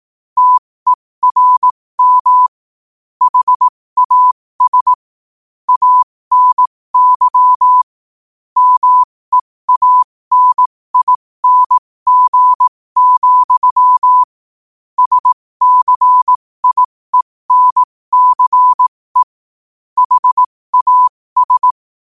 Morse code, heard as a keyed tone, about 18 wpm, spaced as for 15 wpm, 1000 Hz. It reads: TERM HAS ANY MEANING, SCIENCE HAS
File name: qotdfiles / 15wpm